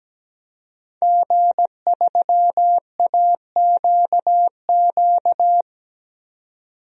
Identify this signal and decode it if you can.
Morse code, heard as a keyed tone, about 17 wpm, 700 Hz. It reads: G3AQQ
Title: DXCC-100 Call Signs - 5 Characters - 17wpm